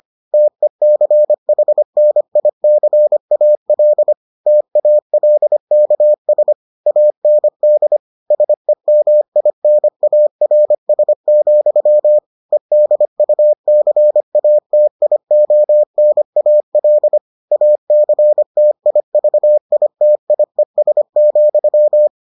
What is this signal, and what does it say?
Morse code, heard as a keyed tone, about 25 wpm, 600 Hz. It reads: TECHNICAL TALKS AND SEMINARS, EDUCATIONAL ACTIVITIES,